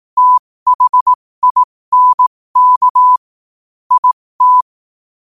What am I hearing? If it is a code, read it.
Morse code, heard as a keyed tone, about 18 wpm, spaced as for 15 wpm, 1000 Hz. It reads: THINK IT
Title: 15 WPM morse code quotes for Fri, 01 Mar 2024